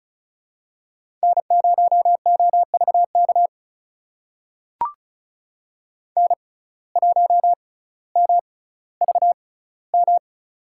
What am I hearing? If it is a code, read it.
Morse code, heard as a keyed tone, about 36 wpm, spaced as for 18 wpm, 700 Hz. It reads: N0OVK N1MVM
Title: USA Call Signs - 5 Characters - 6x Character Spacing - 35wpm